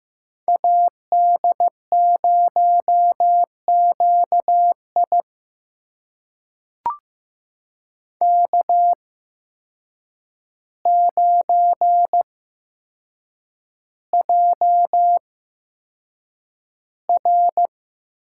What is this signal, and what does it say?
Morse code, heard as a keyed tone, about 15 wpm, spaced as for 7 wpm, 700 Hz. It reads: AD0QI K9JR
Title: USA Call Signs - 8x Character Spacing - 15wpm